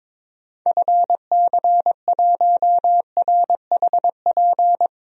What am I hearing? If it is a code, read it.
Morse code, heard as a keyed tone, about 22 wpm, 700 Hz. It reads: FC1RHP
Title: DXCC-100 Call Signs - 6 Characters - Repeated 3x - 22wpm